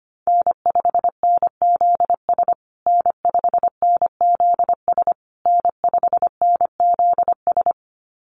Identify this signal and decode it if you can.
Morse code, heard as a keyed tone, about 25 wpm, 700 Hz. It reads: N5NZS N5NZS N5NZS
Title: USA Call Signs Repeated 3x 25wpm